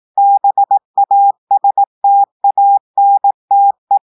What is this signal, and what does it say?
Morse code, heard as a keyed tone, about 18 wpm, 800 Hz. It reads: BASTANTE
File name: audios / portugais / portugais texte en clair 1 vitesse 18 mots minutes